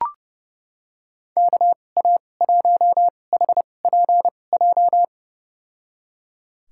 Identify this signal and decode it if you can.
Morse code, heard as a keyed tone, about 32 wpm, spaced as for 22 wpm, 700 Hz. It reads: KA1HPJ
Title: USA Call Signs - 2x Character Spacing - 30wpm